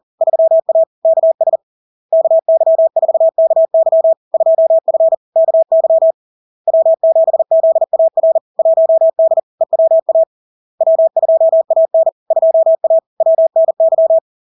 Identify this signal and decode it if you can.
Morse code, heard as a keyed tone, about 40 wpm, 650 Hz. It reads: 3A KS KY4KY 2F KY W7ZAR 1D EWA W2AN 2A WNY
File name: LARK-FD-2022-Log-in-CW-40wpm-650-Hz